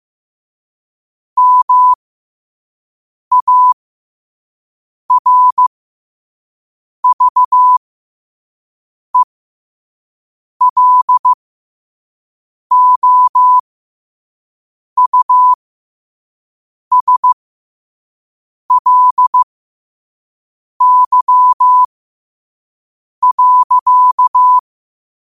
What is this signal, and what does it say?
Morse code, heard as a keyed tone, about 15 wpm, spaced as for 5 wpm, 1000 Hz. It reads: MARVELOUSLY.
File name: qotdfiles / 5wpm